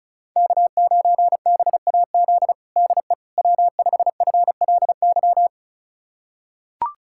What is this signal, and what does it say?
Morse code, heard as a keyed tone, about 36 wpm, 700 Hz. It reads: K9BAZ DE W5FLY
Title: USA Callsign DE Callsign - 5 Characters - 35wpm